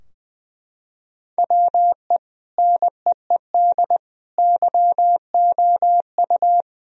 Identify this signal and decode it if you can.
Morse code, heard as a keyed tone, about 20 wpm, 700 Hz. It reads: WE NEED YOU
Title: Sentences from Top 1000 Words - Review Part 3 - 20wpm